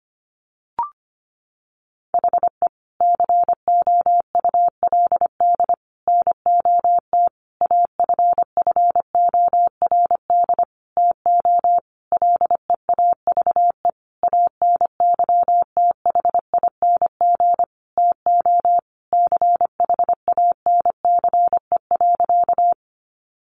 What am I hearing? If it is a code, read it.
Morse code, heard as a keyed tone, about 25 wpm, 700 Hz. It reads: HE COULD NOT AFFORD TO LEAVE ANYTHING TO CHANCE.
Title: Sentences from Top 1100 Words 25wpm